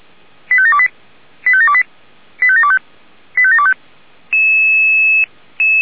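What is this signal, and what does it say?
Five-tone selective call (ZVEI), heard as quick runs of five tones, tones of about 70 ms. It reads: 85728 85728 85725 85725
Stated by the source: Received on an SDR, ZVEI five-tone selective call, code 85753.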